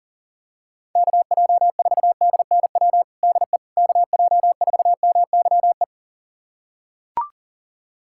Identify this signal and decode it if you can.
Morse code, heard as a keyed tone, about 40 wpm, 700 Hz. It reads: KJ4DNW DE KJ4MYE
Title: USA Callsign DE Callsign - 6 Characters - 40wpm